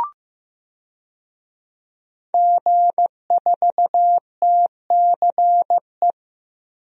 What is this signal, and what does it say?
Morse code, heard as a keyed tone, about 15 wpm, 700 Hz. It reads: G4TCE
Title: DXCC-100 Call Signs - 5 Characters - Repeated 3x - 15wpm